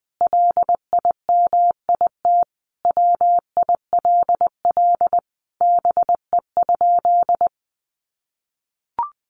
Morse code, heard as a keyed tone, about 20 wpm, 700 Hz. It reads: LIMIT WILL BE?
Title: Sentences from Top 1500 Words 20wpm